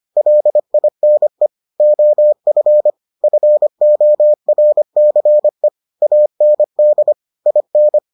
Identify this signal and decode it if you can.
Morse code, heard as a keyed tone, about 25 wpm, 600 Hz. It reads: LINE OF FORCE AND IN